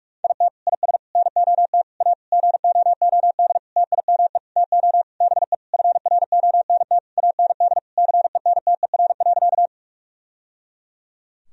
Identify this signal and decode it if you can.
Morse code, heard as a keyed tone, about 46 wpm, 700 Hz. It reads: IT IS NOT A GOOD TIME TO BE FRONT AND CENTER.
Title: Sentences from Top 1000 Words - Review Part 2 - 45wpm